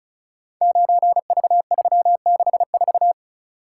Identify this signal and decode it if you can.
Morse code, heard as a keyed tone, about 34 wpm, 700 Hz. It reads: 9V364
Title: Code Groups - 5 Letter:Numbers - 35wpm